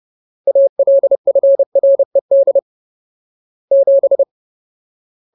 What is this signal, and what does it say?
Morse code, heard as a keyed tone, about 30 wpm, 550 Hz. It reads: ALFRED 7
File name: Ultimate Word List / 30WPM Ultimate